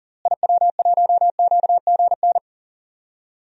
Morse code, heard as a keyed tone, about 40 wpm, 700 Hz. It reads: IW1QGN
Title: DXCC-100 Call Signs - 6 Characters - Repeated 2x - 40wpm